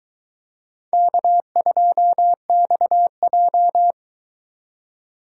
Morse code, heard as a keyed tone, about 22 wpm, 700 Hz. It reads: K2XJ